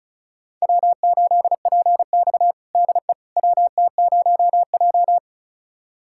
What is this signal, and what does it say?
Morse code, heard as a keyed tone, about 36 wpm, 700 Hz. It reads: W8PX DE WT0J
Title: USA Callsign DE Callsign - 4 Characters - 35wpm